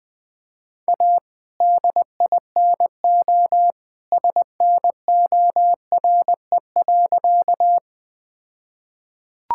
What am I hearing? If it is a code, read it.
Morse code, heard as a keyed tone, about 20 wpm, 700 Hz. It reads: A DINO SNORE.
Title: All-Age Appropriate Jokes - 20wpm